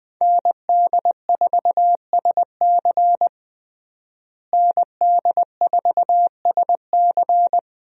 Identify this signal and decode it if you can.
Morse code, heard as a keyed tone, about 20 wpm, 700 Hz. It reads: ND4SC ND4SC
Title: USA Call Signs - 5 Characters - Repeated 2x - 20wpm